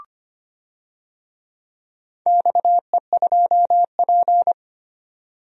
Morse code, heard as a keyed tone, about 25 wpm, 700 Hz. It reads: XE2P
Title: DXCC-100 Call Signs - 4 Characters - 25wpm